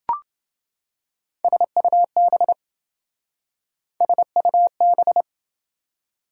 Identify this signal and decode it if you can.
Morse code, heard as a keyed tone, about 27 wpm, 700 Hz. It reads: SUB SUB